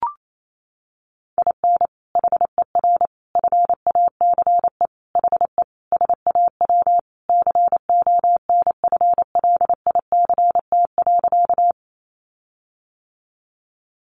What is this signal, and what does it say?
Morse code, heard as a keyed tone, about 28 wpm, 700 Hz. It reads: IN HER FACE HE SAW CONFLICT.
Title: Sentences from Top 1300 Words 28wpm